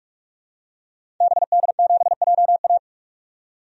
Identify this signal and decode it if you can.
Morse code, heard as a keyed tone, about 45 wpm, 700 Hz. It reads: BD7JA
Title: DXCC-100 Call Signs - 5 Characters - 45wpm